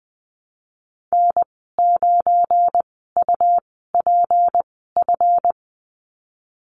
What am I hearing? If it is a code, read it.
Morse code, heard as a keyed tone, about 20 wpm, spaced as for 15 wpm, 700 Hz. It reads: N9UPF
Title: 179 Lesson 29g - Slash - Call Signs 20wpm